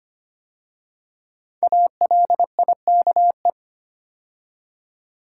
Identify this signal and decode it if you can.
Morse code, heard as a keyed tone, about 25 wpm, 700 Hz. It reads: ALIKE
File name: 5 Letter Words - First 3 Letters Given - 25wpm